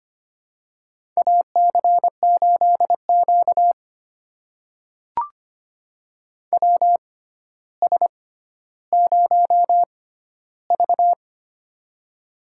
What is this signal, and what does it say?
Morse code, heard as a keyed tone, about 25 wpm, spaced as for 13 wpm, 700 Hz. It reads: AC8Q WS0V
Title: USA Call Signs - 4 Characters - 6x Character Spacing - 25wpm